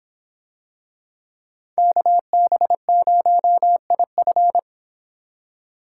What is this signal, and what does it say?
Morse code, heard as a keyed tone, about 25 wpm, 700 Hz. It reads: KB0IF